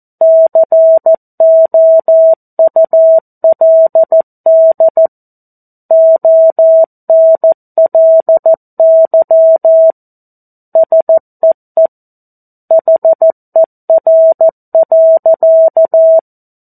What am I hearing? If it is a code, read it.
Morse code, heard as a keyed tone, about 14 wpm, 650 Hz. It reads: COULD ONLY SEE HER.
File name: 02-Chapter-02 14 WPM